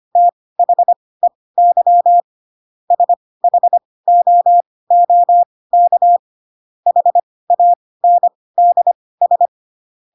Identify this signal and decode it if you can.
Morse code, heard as a keyed tone, about 25 wpm, spaced as for 18 wpm, 700 Hz. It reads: THEY SHOOK HANDS
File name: Pinocchio / Pinocchio ch2 12 25 ik1hge